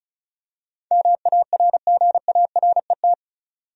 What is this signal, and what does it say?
Morse code, heard as a keyed tone, about 34 wpm, 700 Hz. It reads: MARGARET
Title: USA Common Names - 35wpm